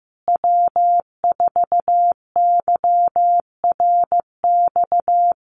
Morse code, heard as a keyed tone, about 15 wpm, 700 Hz. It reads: W4YRX
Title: Common QSO Phrases Encore 15wpm